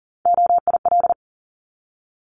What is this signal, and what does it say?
Morse code, heard as a keyed tone, about 41 wpm, 700 Hz. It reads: OIL
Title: Multi-letter Phonemes 40wpm